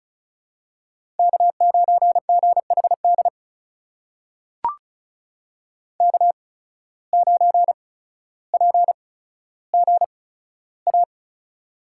Morse code, heard as a keyed tone, about 34 wpm, spaced as for 15 wpm, 700 Hz. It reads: K9GHD K9PGA
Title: USA Call Signs - 5 Characters - 8x Character Spacing - 35wpm